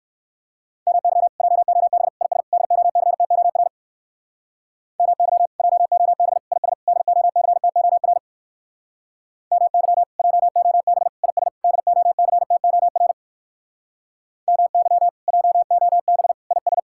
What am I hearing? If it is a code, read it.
Morse code, heard as a keyed tone, about 56 wpm, 700 Hz. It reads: MY JOB IS DOCTOR MY JOB IS DOCTOR MY JOB IS DOCTOR MY JOB IS